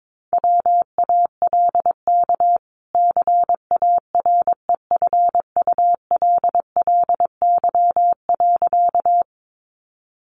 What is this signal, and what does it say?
Morse code, heard as a keyed tone, about 22 wpm, 700 Hz. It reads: WALK CAREFULLY.
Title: Sentences from Top 1300 Words 22wpm